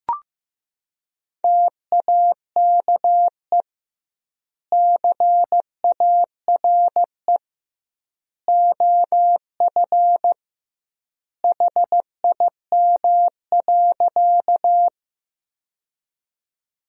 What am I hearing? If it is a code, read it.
Morse code, heard as a keyed tone, about 15 wpm, 700 Hz. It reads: TAKE CARE OF HIM.
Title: 2x Word-Spacing - Sentences from Top 300 Words - 15wpm